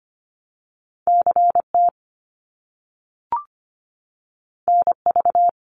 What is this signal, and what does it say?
Morse code, heard as a keyed tone, about 25 wpm, 700 Hz. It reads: CT NV
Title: US States Repeated 3x 25wpm